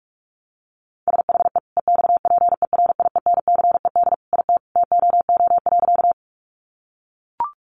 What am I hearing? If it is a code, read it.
Morse code, heard as a keyed tone, about 44 wpm, 700 Hz. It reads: SHE EXPERIENCED IT TOO.
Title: Sentences from Top 1100 Words 45wpm